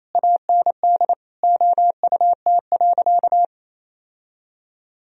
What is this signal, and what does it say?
Morse code, heard as a keyed tone, about 28 wpm, 700 Hz. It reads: AND OUT.
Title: Sentences from Top 1000 Words - Review Part 2 - 28wpm